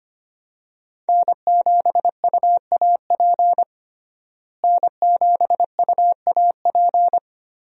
Morse code, heard as a keyed tone, about 25 wpm, 700 Hz. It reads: N7UAP N7UAP
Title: USA Call Signs - 5 Characters - Repeated 3x - 25wpm